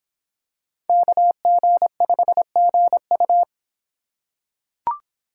Morse code, heard as a keyed tone, about 26 wpm, 700 Hz. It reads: KG5GU